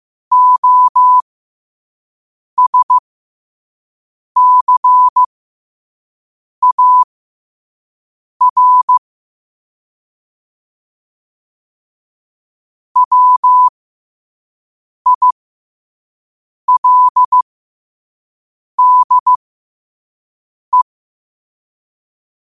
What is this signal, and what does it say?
Morse code, heard as a keyed tone, about 15 wpm, spaced as for 5 wpm, 1000 Hz. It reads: OSCAR WILDE
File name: qotdfiles / 5wpm